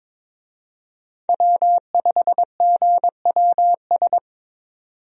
Morse code, heard as a keyed tone, about 22 wpm, 700 Hz. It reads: W5GWS
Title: USA Call Signs - 5 Characters - 22wpm